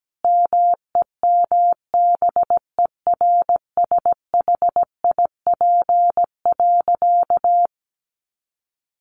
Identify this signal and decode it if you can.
Morse code, heard as a keyed tone, about 17 wpm, 700 Hz. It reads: MEMBERSHIP.
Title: Sentences from Top 1400 Words 17wpm